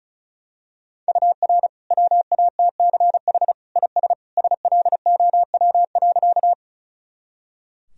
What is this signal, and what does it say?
Morse code, heard as a keyed tone, about 36 wpm, 700 Hz. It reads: UR WATCH IS SLOW.
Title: Sentences from Top 400 Words and Abbreviations 35wpm